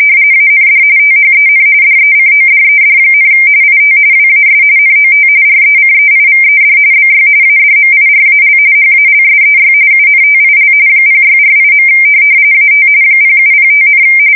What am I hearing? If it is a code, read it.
RTTY, about 45.45 baud. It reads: AND THEN DECODED, IT WAS NOT RECEIVED BY ADMIRAL HALSEY UNTIL 0648. AT 0705,
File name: fsk170 / USS-New-Jersey